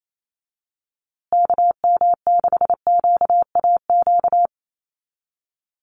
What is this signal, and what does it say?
Morse code, heard as a keyed tone, about 28 wpm, 700 Hz. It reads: KM6QAQ
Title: USA Call Signs Repeated 3x 28wpm